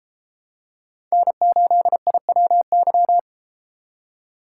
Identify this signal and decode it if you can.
Morse code, heard as a keyed tone, about 33 wpm, 700 Hz. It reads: N8IWY